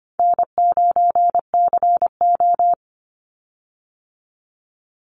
Morse code, heard as a keyed tone, about 25 wpm, 700 Hz. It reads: N9CO
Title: Call Signs - 25wpm